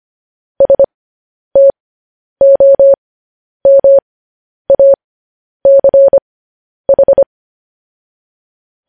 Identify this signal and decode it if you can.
Morse code, heard as a keyed tone, about 25 wpm, spaced as for 10 wpm, 550 Hz. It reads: STOMACH